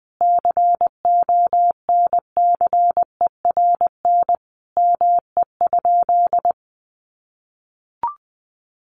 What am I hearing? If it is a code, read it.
Morse code, heard as a keyed tone, about 20 wpm, 700 Hz. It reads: CONCERN ME?
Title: Sentences from Top 1300 Words 20wpm